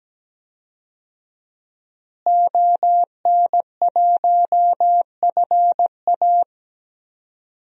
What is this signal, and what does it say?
Morse code, heard as a keyed tone, about 17 wpm, 700 Hz. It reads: ON1FA
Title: DXCC-100 Call Signs - 5 Characters - Repeated 2x - 17wpm